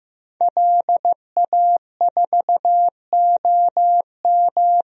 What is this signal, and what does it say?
Morse code, heard as a keyed tone, about 15 wpm, 700 Hz. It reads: LA4OM
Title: DXCC-100 Call Signs - 5 Characters - Repeated 2x - 15wpm